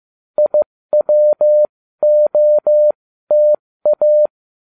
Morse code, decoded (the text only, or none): IWOTA